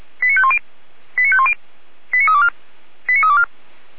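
Five-tone selective call (ZVEI), heard as quick runs of five tones, tones of about 70 ms. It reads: 89510 89510 89324 89324
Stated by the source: ZVEI five-tone call received by SDR, station feuerwehr-orlamuende.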